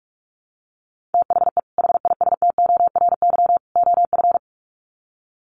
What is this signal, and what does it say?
Morse code, heard as a keyed tone, about 44 wpm, 700 Hz. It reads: THE HISTORY OF
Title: Sets of 3 Words 45wpm